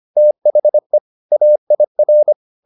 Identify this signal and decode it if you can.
Morse code, heard as a keyed tone, about 25 wpm, 600 Hz. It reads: THE AIR
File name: the old man and the sea part 1 25wpm